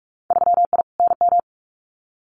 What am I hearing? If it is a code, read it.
Morse code, heard as a keyed tone, about 46 wpm, 690 Hz. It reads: 3I NM
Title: Winter Field Day Exchanges Encore 45wpm